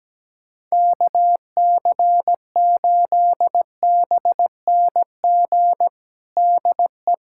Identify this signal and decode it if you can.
Morse code, heard as a keyed tone, about 17 wpm, 700 Hz. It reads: KC8BNG DE
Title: USA Callsign DE Callsign - 6 Characters - 17wpm